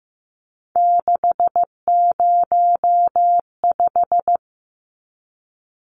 Morse code, heard as a keyed tone, about 15 wpm, 700 Hz. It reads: 605
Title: Numbers - 3-Digit 15wpm